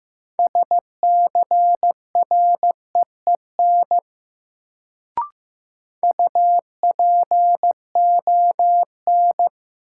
Morse code, heard as a keyed tone, about 15 wpm, 700 Hz. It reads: SCREEN UPON
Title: Sentences from Top 1000 Words - Review Part 6 - New Word Drill - 15wpm